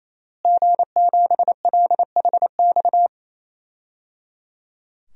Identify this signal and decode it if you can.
Morse code, heard as a keyed tone, about 28 wpm, 700 Hz. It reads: G7LHX
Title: DXCC-100 Call Signs - 5 Characters - Repeated 3x - 28wpm